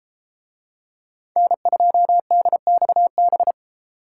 Morse code, heard as a keyed tone, about 33 wpm, 700 Hz. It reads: N2DXB